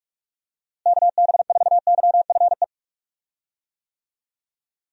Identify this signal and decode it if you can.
Morse code, heard as a keyed tone, about 46 wpm, 700 Hz. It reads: KB4YFE
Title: USA Call Signs - 6 Characters - Repeated 2x - 45wpm